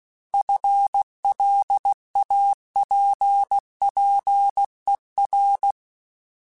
Morse code, heard as a keyed tone, about 16 wpm, 800 Hz. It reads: FLAPPER
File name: FL Runs / FL Runs mp3s / FL Runs @15